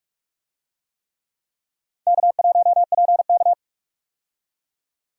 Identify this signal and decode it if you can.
Morse code, heard as a keyed tone, about 46 wpm, 700 Hz. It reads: K1PK
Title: USA Call Signs - 4 Characters - 45wpm